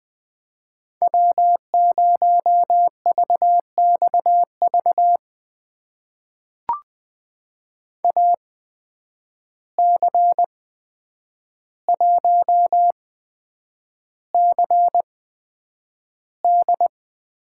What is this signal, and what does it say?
Morse code, heard as a keyed tone, about 20 wpm, spaced as for 9 wpm, 700 Hz. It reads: W0VXV AC1CD
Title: USA Call Signs - 5 Characters - 8x Character Spacing - 20wpm